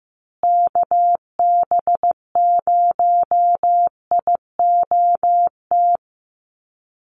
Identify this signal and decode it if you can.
Morse code, heard as a keyed tone, about 15 wpm, 700 Hz. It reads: KB0IOT
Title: USA Call Signs Repeated 2x 15wpm